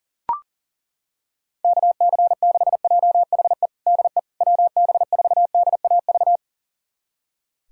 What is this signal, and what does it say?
Morse code, heard as a keyed tone, about 40 wpm, 700 Hz. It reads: KC6JHE DE WB4DAV
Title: USA Callsign DE Callsign 40wpm